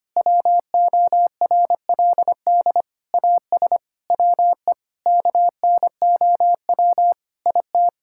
Morse code, heard as a keyed tone, about 25 wpm, 700 Hz. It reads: WORLD AS WE KNOW IT
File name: Sets of 9 Words - 25wpm